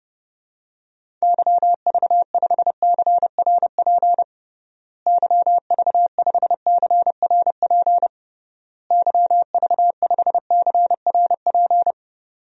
Morse code, heard as a keyed tone, about 32 wpm, 700 Hz. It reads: YV5CRP YV5CRP YV5CRP
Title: DXCC-100 Call Signs - 6 Characters - Repeated 3x - 30wpm